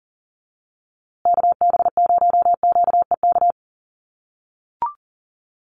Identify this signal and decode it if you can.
Morse code, heard as a keyed tone, about 40 wpm, 700 Hz. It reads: KB0QEK
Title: USA Call Signs Repeated 2x 40wpm